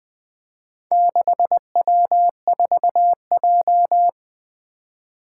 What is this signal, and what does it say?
Morse code, heard as a keyed tone, about 21 wpm, 700 Hz. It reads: 6W4J
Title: Four Letter-Number Combinations - 20wpm